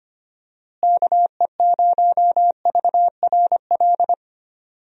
Morse code, heard as a keyed tone, about 25 wpm, 700 Hz. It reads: KE0VRL
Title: USA Call Signs - 6 Characters - Repeated 3x - 25wpm